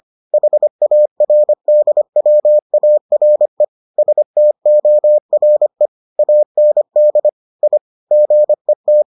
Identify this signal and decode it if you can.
Morse code, heard as a keyed tone, about 25 wpm, 600 Hz. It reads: HARDWARE STORE AND I GET